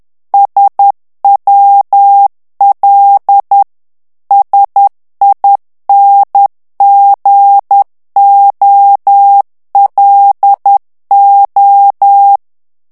Morse code, heard as a keyed tone, about 10 wpm, 800 Hz. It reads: SWL SINGOLO
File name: ITALIEN texte clair 2 vitesse 10 mots minute